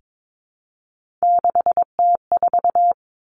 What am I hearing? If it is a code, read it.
Morse code, heard as a keyed tone, about 22 wpm, 700 Hz. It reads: 6T4
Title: Three Challenging Letter-Number Combinations 22wpm